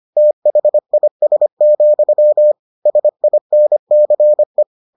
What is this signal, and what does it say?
Morse code, heard as a keyed tone, about 25 wpm, 600 Hz. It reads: THIS, SINCE